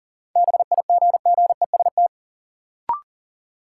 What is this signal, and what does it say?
Morse code, heard as a keyed tone, about 40 wpm, 700 Hz. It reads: BIGGEST
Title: Sets of 4 Words - New Word Drill - 40wpm